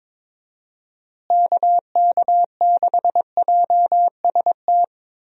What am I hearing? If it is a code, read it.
Morse code, heard as a keyed tone, about 22 wpm, 700 Hz. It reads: KK6JST
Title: USA Call Signs - 6 Characters - Repeated 2x - 22wpm